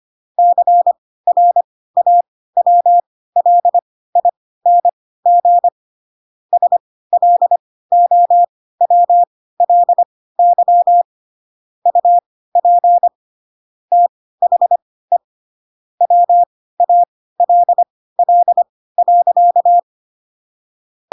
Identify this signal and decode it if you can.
Morse code, heard as a keyed tone, about 25 wpm, spaced as for 16 wpm, 700 Hz. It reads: CRAWLING SLOWLY UP THE WALL.
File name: Pinocchio / Pinocchio ch4 10 25 ik1hge